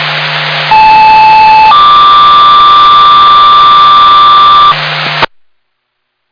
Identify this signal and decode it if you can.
Two-tone page, heard as a one-second tone, then a three-second tone, about 870 then 1180 Hz, with noise. It is heard from 0.7 to 4.7 s.